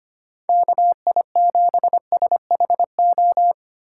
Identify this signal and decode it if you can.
Morse code, heard as a keyed tone, about 25 wpm, 700 Hz. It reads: KI7SHO
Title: USA Call Signs - 6 Characters - 4x Character Spacing - 25wpm